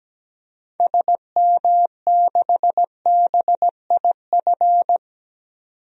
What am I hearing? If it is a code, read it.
Morse code, heard as a keyed tone, about 17 wpm, 700 Hz. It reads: SM6BIF
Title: DXCC-100 Call Signs - 6 Characters - 17wpm